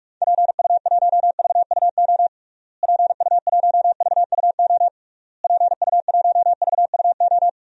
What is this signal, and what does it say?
Morse code, heard as a keyed tone, about 46 wpm, 700 Hz. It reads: PU1VUO PU1VUO PU1VUO
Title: DXCC-100 Call Signs - 6 Characters - Repeated 3x - 45wpm